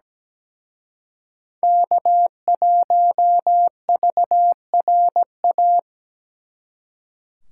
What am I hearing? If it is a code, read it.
Morse code, heard as a keyed tone, about 20 wpm, 700 Hz. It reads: K1VRA
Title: Speed Racing - USA 1x3 Call Signs 17wpm